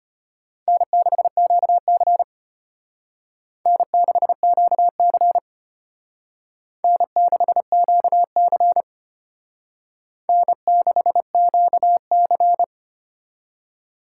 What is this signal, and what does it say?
Morse code, heard as a keyed tone, about 32 wpm, 700 Hz. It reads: N6QC N6QC N6QC N6QC